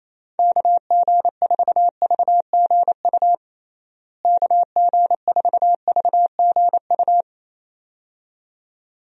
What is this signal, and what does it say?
Morse code, heard as a keyed tone, about 28 wpm, 700 Hz. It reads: KG4VGU KG4VGU
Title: USA Call Signs - 6 Characters - Repeated 2x - 28wpm